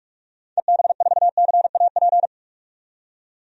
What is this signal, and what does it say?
Morse code, heard as a keyed tone, about 42 wpm, 700 Hz. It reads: EB4CAP
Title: DXCC-100 Call Signs - 6 Characters - 45wpm